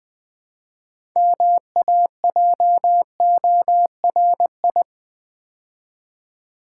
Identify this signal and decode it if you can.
Morse code, heard as a keyed tone, about 20 wpm, 700 Hz. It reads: MAJORI
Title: Build A Word - 3 Letters At A Time - 8-Letter Words - 20wpm